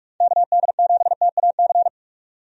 Morse code, heard as a keyed tone, about 46 wpm, 700 Hz. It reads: KD7TAC
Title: USA Call Signs - 6 Characters - 45wpm